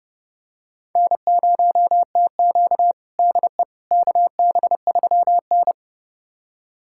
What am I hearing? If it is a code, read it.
Morse code, heard as a keyed tone, about 30 wpm, 700 Hz. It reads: N0TQ DE KB3N
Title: USA Callsign DE Callsign - 4 Characters - 30wpm